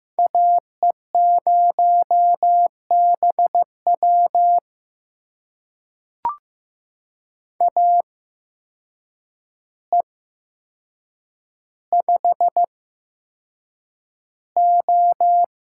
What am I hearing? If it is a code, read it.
Morse code, heard as a keyed tone, about 15 wpm, spaced as for 7 wpm, 700 Hz. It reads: AE0BW AE5O
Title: USA Call Signs - 5 Characters - 8x Character Spacing - 15wpm